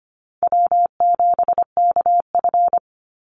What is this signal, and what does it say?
Morse code, heard as a keyed tone, about 25 wpm, 700 Hz. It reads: W7KF
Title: Easy Call Signs - 25wpm